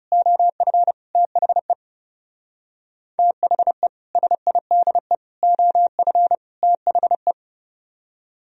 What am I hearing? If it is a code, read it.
Morse code, heard as a keyed tone, about 36 wpm, 700 Hz. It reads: OF THE THE SIDE OF THE
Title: Speed-Racing - Sets of 4 Words 30wpm